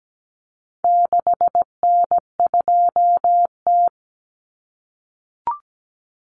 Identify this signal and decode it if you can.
Morse code, heard as a keyed tone, about 17 wpm, 700 Hz. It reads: 6N2T
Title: Common Cut Numbers - 4-Digit 17wpm